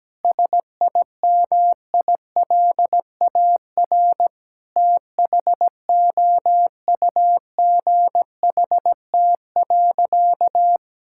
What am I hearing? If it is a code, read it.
Morse code, heard as a keyed tone, about 17 wpm, 700 Hz. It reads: SIMILAR THOUGHT.
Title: Sentences from Top 1000 Words - Review Part 5 - 17wpm